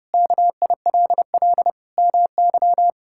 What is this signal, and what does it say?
Morse code, heard as a keyed tone, about 32 wpm, 700 Hz. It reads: KILL MY
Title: Sentences from Top 1600 Words 30wpm